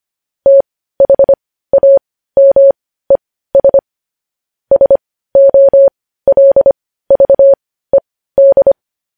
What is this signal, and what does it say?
Morse code, heard as a keyed tone, about 25 wpm, spaced as for 15 wpm, 550 Hz. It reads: THAMES SOLVED
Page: news-headlines-in-morse-code-at-15-wpm